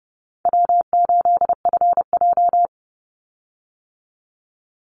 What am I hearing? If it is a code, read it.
Morse code, heard as a keyed tone, about 30 wpm, 700 Hz. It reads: W8FJ
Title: Easy Call Signs - 30wpm